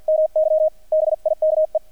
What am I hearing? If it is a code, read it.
Morse code, heard as a keyed tone, about 48 wpm, 620 Hz, with radio noise. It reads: MY BIKE
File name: archived / CRSnet